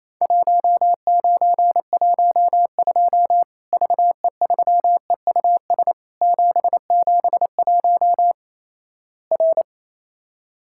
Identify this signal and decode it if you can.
Morse code, heard as a keyed tone, about 28 wpm, 700 Hz. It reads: 1912 VE3EUS 771 R
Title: Worked All Europe DX Contest - QTC Exchange - 28wpm